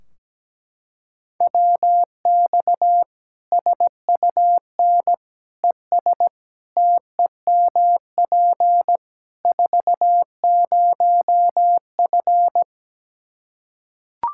Morse code, heard as a keyed tone, about 17 wpm, 700 Hz. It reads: WX SUN ES TEMP 40F
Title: Practice QSO - Exchange 2 - Info - 17wpm